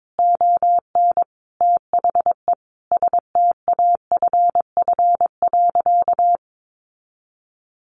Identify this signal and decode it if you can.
Morse code, heard as a keyed tone, about 22 wpm, 700 Hz. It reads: ON THE STAFF.
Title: Sentences from Top 1200 Words 22wpm